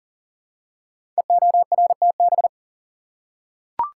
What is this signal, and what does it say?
Morse code, heard as a keyed tone, about 41 wpm, 700 Hz. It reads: EORTB
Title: Code Groups - 5 Letter:Numbers - 40wpm